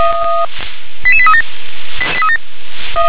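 Five-tone selective call (ZVEI), heard as quick runs of five tones, tones of about 70 ms. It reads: 88837
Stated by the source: ZVEI five-tone selective call received on an SDR.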